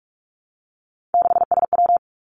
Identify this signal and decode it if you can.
Morse code, heard as a keyed tone, about 46 wpm, 700 Hz. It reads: 6SW
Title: Three Challenging Letter-Number Combinations 45wpm